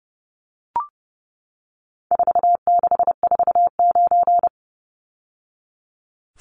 Morse code, heard as a keyed tone, about 30 wpm, 700 Hz. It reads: 4649
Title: Numbers - 4-Digit 30wpm